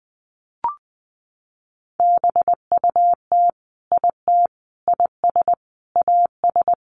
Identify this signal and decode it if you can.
Morse code, heard as a keyed tone, about 19 wpm, 700 Hz. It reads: BUT IT IS AS
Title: Sentences from Top 1500 Words 20wpm